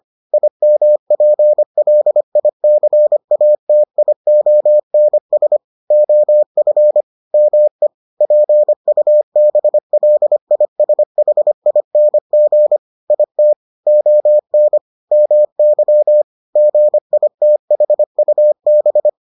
Morse code, heard as a keyed tone, about 25 wpm, 600 Hz. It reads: IMPLICATIONS OF ME PUBLISHING IT ON MY GITHUB